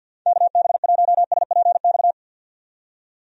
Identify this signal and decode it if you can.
Morse code, heard as a keyed tone, about 46 wpm, 700 Hz. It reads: KB1SPX
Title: USA Call Signs - 6 Characters - 50wpm